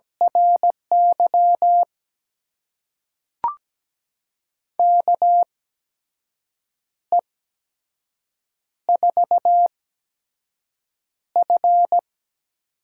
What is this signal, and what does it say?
Morse code, heard as a keyed tone, about 17 wpm, spaced as for 6 wpm, 700 Hz. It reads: RY KE4F
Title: USA Call Signs - 6 Characters - 8x Character Spacing - 17wpm